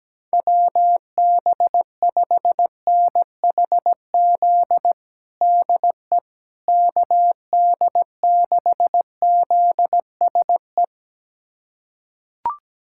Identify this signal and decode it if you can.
Morse code, heard as a keyed tone, about 17 wpm, 700 Hz. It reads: WB5NHZ DE KD6ZSE
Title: USA Callsign DE Callsign 17wpm